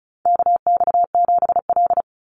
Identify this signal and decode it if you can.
Morse code, heard as a keyed tone, about 34 wpm, 700 Hz. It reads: KX7L
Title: NA SOTA Chaser Callsigns 35wpm